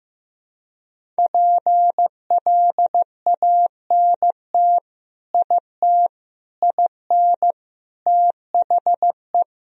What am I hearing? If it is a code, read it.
Morse code, heard as a keyed tone, about 15 wpm, 700 Hz. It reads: PLANT IT IN THE
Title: Sentences from Top 1000 Words - Review Part 1 - 15wpm